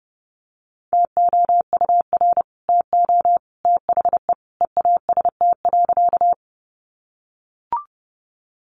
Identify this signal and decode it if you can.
Morse code, heard as a keyed tone, about 30 wpm, 700 Hz. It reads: TOUR TO THE EAST.
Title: Sentences from Top 1400 Words 30wpm